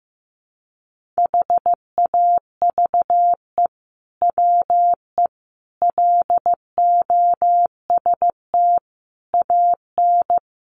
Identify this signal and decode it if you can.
Morse code, heard as a keyed tone, about 15 wpm, 700 Hz. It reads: HAVE WE LOST AN
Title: Sentences from Top 1400 Words 15wpm